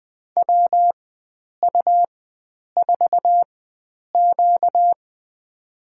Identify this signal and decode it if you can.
Morse code, heard as a keyed tone, about 20 wpm, spaced as for 9 wpm, 700 Hz. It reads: WU4Q
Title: USA Call Signs - 4 Characters - 4x Character Spacing - 20wpm